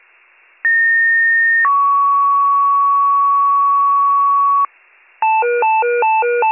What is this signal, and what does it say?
Two-tone page, heard as a one-second tone, then a three-second tone, about 1850 then 1120 Hz, with noise. It is heard from 0.6 to 4.7 s.